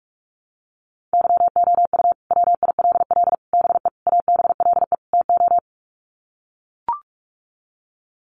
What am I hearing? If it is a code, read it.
Morse code, heard as a keyed tone, about 44 wpm, 700 Hz. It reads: YOU WILL BE ABLE TO
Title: Sets of 5 Words Encore2 45wpm